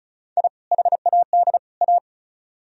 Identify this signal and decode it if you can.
Morse code, heard as a keyed tone, about 46 wpm, 700 Hz. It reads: I HAD A